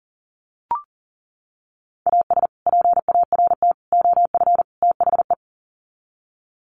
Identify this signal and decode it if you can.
Morse code, heard as a keyed tone, about 40 wpm, 700 Hz. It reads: AS PART OF THE
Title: Sets of 4 Words 40wpm